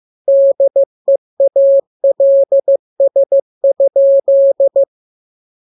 Morse code, heard as a keyed tone, about 15 wpm, 550 Hz. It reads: DEALS?